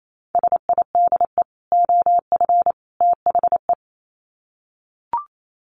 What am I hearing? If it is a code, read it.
Morse code, heard as a keyed tone, about 28 wpm, 700 Hz. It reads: SIDE OF THE
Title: Sets of 3 Words 28wpm